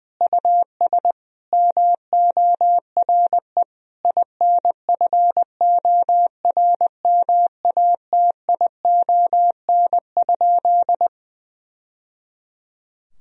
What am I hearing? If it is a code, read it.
Morse code, heard as a keyed tone, about 21 wpm, 700 Hz. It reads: US MORE INFORMATION?
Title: Sentences from Top 1000 Words - Review Part 3 - 20wpm